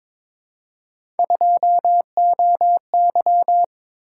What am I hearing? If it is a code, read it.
Morse code, heard as a keyed tone, about 22 wpm, 700 Hz. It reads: 2OY
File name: practice / 22wpm / Three Letter-Number Combinations - 22wpm